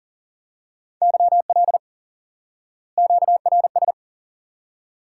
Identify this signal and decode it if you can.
Morse code, heard as a keyed tone, about 40 wpm, 700 Hz. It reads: YL QRS
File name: IQE Course 3 - Lesson 89 - YAGI - Two Elements - 40wpm